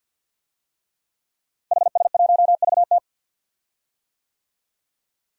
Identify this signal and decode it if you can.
Morse code, heard as a keyed tone, about 51 wpm, 700 Hz. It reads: HS1VT
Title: DXCC-100 Call Signs - 5 Characters - Repeated 3x - 50wpm